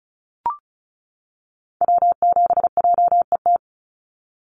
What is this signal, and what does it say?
Morse code, heard as a keyed tone, about 34 wpm, 700 Hz. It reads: W7JET
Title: NA SOTA Activator Callsigns 35wpm